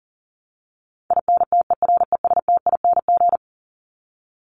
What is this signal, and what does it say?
Morse code, heard as a keyed tone, about 40 wpm, 700 Hz. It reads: INTERESTING
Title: ING Suffix - Top 100 Words with Base Words 40wpm